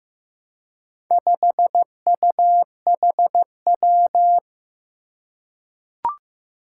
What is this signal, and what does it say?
Morse code, heard as a keyed tone, about 15 wpm, 700 Hz. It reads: 5UHW
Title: Four Challenging Letter-Number Combinations 15wpm